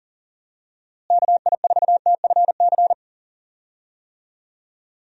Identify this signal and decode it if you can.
Morse code, heard as a keyed tone, about 40 wpm, 700 Hz. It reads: KI4TFC
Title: USA Call Signs - 6 Characters - Repeated 3x - 40wpm